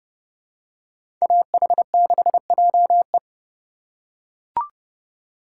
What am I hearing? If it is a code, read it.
Morse code, heard as a keyed tone, about 30 wpm, 700 Hz. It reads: AH6JE
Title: USA Call Signs - 5 Characters - 4x Character Spacing - 30wpm